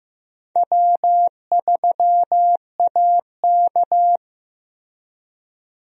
Morse code, heard as a keyed tone, about 15 wpm, 700 Hz. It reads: W3AK
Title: USA Call Signs - 4 Characters - Repeated 2x - 15wpm